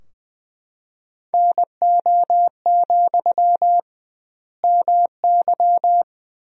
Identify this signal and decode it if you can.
Morse code, heard as a keyed tone, about 20 wpm, 700 Hz. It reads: NO, MY
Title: 2x Word-Spacing - Sentences from Top 400 Words - 20wpm